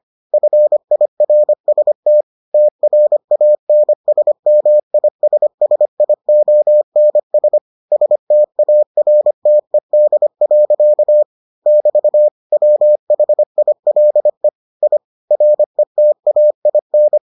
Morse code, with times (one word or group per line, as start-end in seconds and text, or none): 0.3-2.2 FIRST
2.5-7.6 TRANSMISSIONS
7.9-11.2 STARTED.
11.7-12.3 <BT>
12.5-14.5 WHILE
14.8-15.0 I
15.3-17.2 RETAIN